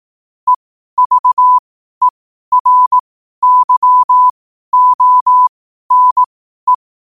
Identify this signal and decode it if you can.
Morse code, heard as a keyed tone, about 18 wpm, spaced as for 12 wpm, 1000 Hz. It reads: EVERYONE
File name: qotdfiles / 12wpm